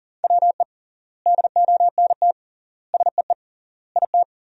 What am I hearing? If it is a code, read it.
Morse code, heard as a keyed tone, about 40 wpm, 700 Hz. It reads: WE DONT SEE IT
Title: 3x Word-Spacing - Sentences from Top 300 Words - 40wpm